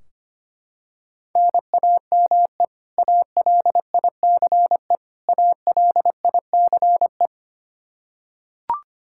Morse code, heard as a keyed tone, about 25 wpm, 700 Hz. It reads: NAME ALICE ALICE
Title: Practice QSO - Exchange 1 - Beginning - 25wpm